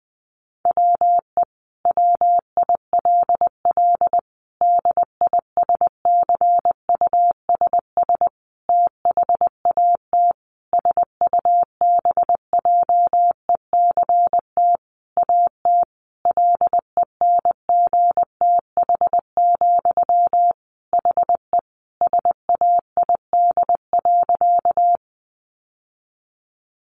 Morse code, heard as a keyed tone, about 20 wpm, 700 Hz. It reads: WE WILL DISCUSS THAT SUBJECT AT LENGTH, HE SAID.
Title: Sentences from Top 1000 Words 20wpm